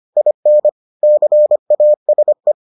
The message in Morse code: IN CASE

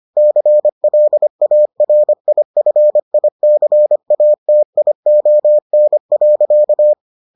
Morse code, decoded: CLARIFICATION.

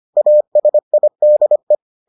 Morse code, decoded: ASIDE

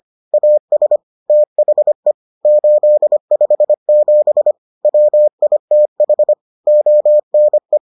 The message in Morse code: AS THE 857 WITH ONE

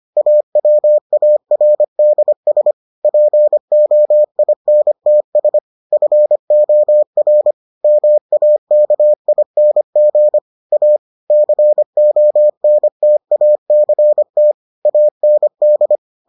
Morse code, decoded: AWARDS POINTS FOR MAKING A CONTACT AND

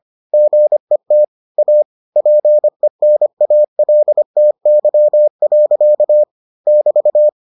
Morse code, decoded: GET A PENALTY. <BT>

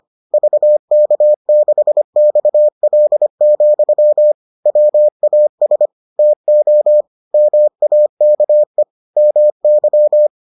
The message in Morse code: VK6XL, WAS TO MAKE MY

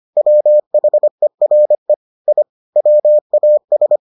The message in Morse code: WHERE I WAS

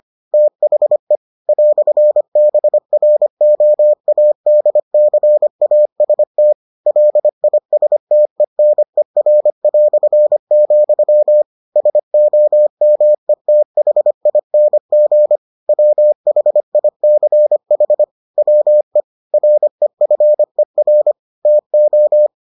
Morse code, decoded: THE "BROADCAST LISTENER", SOMETHING WHICH WE REFER TO

